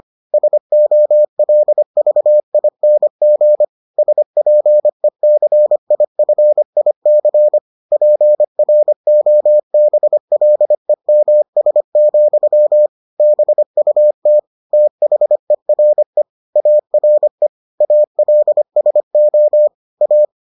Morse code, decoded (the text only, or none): SOLVING SPECIFIC PROBLEMS, BUT THERE ARE ALSO A